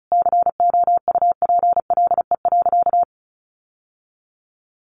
Morse code, heard as a keyed tone, about 34 wpm, 700 Hz. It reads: COUPLE.
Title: Sentences from Top 1100 Words 35wpm